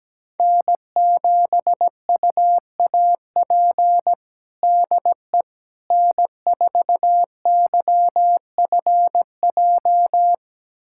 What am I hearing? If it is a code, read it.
Morse code, heard as a keyed tone, about 17 wpm, 700 Hz. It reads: N7UAP DE N4YFJ
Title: USA Callsign DE Callsign 17wpm